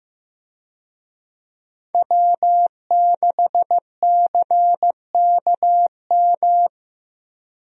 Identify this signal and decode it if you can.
Morse code, heard as a keyed tone, about 15 wpm, 700 Hz. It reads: W6CKM
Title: USA Call Signs - 5 Characters - Repeated 3x - 15wpm